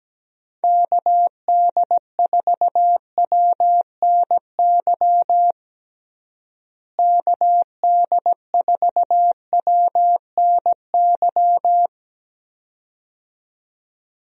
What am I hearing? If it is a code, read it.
Morse code, heard as a keyed tone, about 17 wpm, 700 Hz. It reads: KD4WNY KD4WNY
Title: USA Call Signs - 6 Characters - Repeated 2x - 17wpm